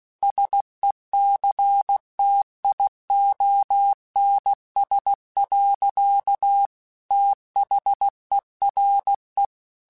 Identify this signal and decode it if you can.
Morse code, heard as a keyed tone, about 16 wpm, 800 Hz. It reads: SECTIONS. THERE
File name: Technical Subjects / Technical Subjects-mp3s / Sw Pwr Sup @15